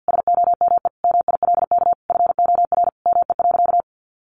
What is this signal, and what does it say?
Morse code, heard as a keyed tone, about 50 wpm, 700 Hz. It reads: SOME MILK FOR ME.
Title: Sentences from Top 1500 Words 50wpm